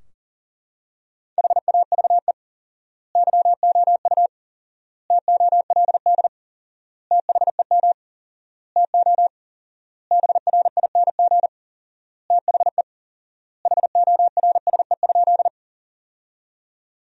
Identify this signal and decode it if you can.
Morse code, heard as a keyed tone, about 40 wpm, 700 Hz. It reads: HAVE YOU TOLD THEM TO BRING THE HORSE?
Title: 4x Word-Spacing - Sentences from Top 500 Words - 40wpm